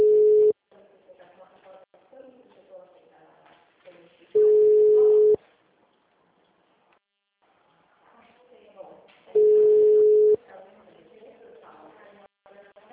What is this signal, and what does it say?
Ringback tone (Europe, 425 Hz), with speech.
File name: out- -2013- - - .324